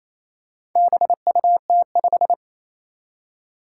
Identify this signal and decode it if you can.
Morse code, heard as a keyed tone, about 27 wpm, 700 Hz. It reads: BUT5
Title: Four Challenging Letter-Number Combinations 28wpm